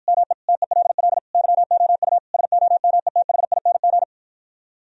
Morse code, heard as a keyed tone, about 53 wpm, 700 Hz. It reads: ME TELL YOU SOMETHING